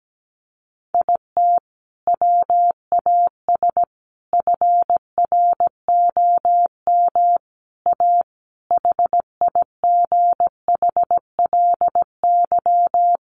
Morse code, heard as a keyed tone, about 17 wpm, 700 Hz. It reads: IT WAS FROM A HIGHLY
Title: Sentences from Top 1100 Words 17wpm